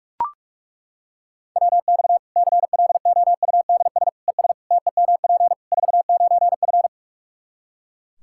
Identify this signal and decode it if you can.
Morse code, heard as a keyed tone, about 41 wpm, 700 Hz. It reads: WX CLOUDS ES TEMP 49F
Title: K1USN SST QSO Practice 45wpm